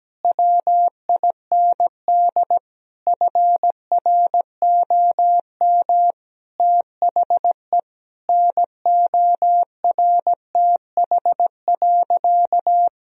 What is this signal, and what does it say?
Morse code, heard as a keyed tone, about 17 wpm, 700 Hz. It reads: WIND FROM THE NORTH.
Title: Sentences from Top 1000 Words - Review Part 2 - 17wpm